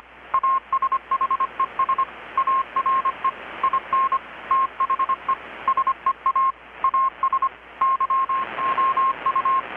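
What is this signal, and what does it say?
Morse code, heard as a keyed tone, about 25 wpm, 1070 Hz. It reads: ASHES ARE IN THE SEA AS YOU
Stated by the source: Morse from KPH's Night of Nights XX maritime radiotelegraphy event, received in Chiapas, Mexico about 3,800 km away on a Kenwood R-600 with a grounded 10 m long-wire antenna.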